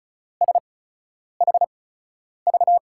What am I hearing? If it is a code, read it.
Morse code, heard as a keyed tone, about 36 wpm, spaced as for 10 wpm, 700 Hz. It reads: SHV
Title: USA Call Signs - 5 Characters - 8x Character Spacing - 35wpm